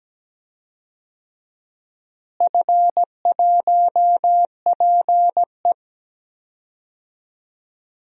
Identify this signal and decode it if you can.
Morse code, heard as a keyed tone, about 17 wpm, 700 Hz. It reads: F1PE
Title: DXCC-100 Call Signs - 4 Characters - Repeated 3x - 17wpm